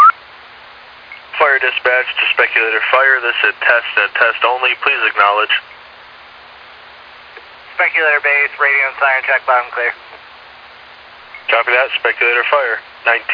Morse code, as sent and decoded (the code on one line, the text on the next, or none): none